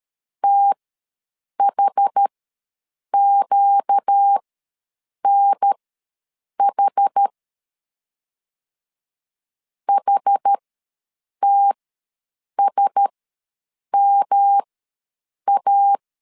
Morse code, heard as a keyed tone, about 13 wpm, spaced as for 7 wpm, 800 Hz. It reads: THQNH HTSMA
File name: LEZIONI / 5 LBVC / 5 L ws 12wpm 2958